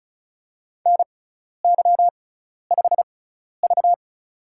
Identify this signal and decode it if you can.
Morse code, heard as a keyed tone, about 36 wpm, spaced as for 12 wpm, 700 Hz. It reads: NY5V
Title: USA Call Signs - 4 Characters - 6x Character Spacing - 35wpm